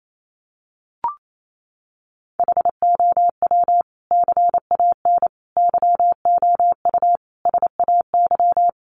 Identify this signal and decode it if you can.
Morse code, heard as a keyed tone, about 28 wpm, 700 Hz. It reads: HOW CAN YOU SAY
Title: Sentences from Top 200 Words 28wpm